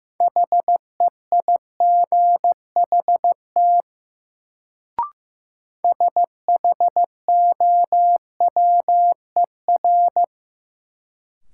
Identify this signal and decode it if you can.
Morse code, heard as a keyed tone, about 15 wpm, 700 Hz. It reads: HEIGHT SHOWER
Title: Sentences from Top 1600 Words - New Word Drill - 15wpm